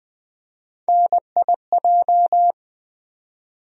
Morse code, heard as a keyed tone, about 20 wpm, 700 Hz. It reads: NIJ
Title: USA Call Sign Postfixes - 3 Letters 20wpm